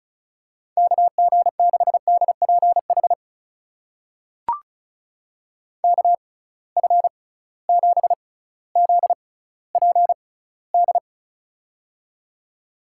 Morse code, heard as a keyed tone, about 36 wpm, spaced as for 18 wpm, 700 Hz. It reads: KG6DPH KF7ZPD
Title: USA Call Signs - 6 Characters - 6x Character Spacing - 35wpm